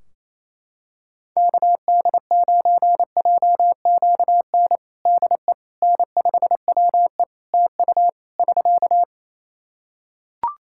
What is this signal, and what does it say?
Morse code, heard as a keyed tone, about 28 wpm, 700 Hz. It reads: KD9JQN DE N5WE TU <SK>
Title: Practice QSO - Exchange 4 - Ending - 28wpm